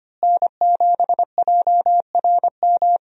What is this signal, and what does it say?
Morse code, heard as a keyed tone, about 25 wpm, 700 Hz. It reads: N7JRM
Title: USA Call Signs - 5 Characters - Repeated 2x - 25wpm